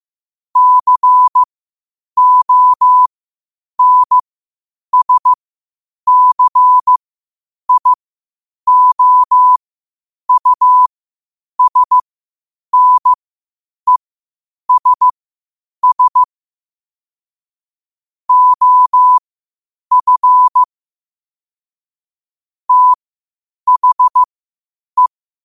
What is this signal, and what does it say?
Morse code, heard as a keyed tone, about 15 wpm, spaced as for 8 wpm, 1000 Hz. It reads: CONSCIOUSNESS OF THE
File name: qotdfiles / 8wpm